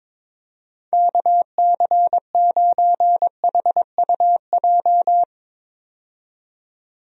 Morse code, heard as a keyed tone, about 22 wpm, 700 Hz. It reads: KC9HUJ
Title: USA Call Signs - 6 Characters - 22wpm